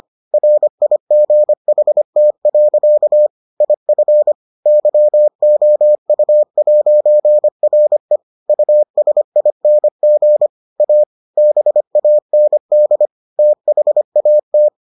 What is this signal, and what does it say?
Morse code, heard as a keyed tone, about 25 wpm, 600 Hz. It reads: RIGHT. IF YOU'RE USING A BAND THAT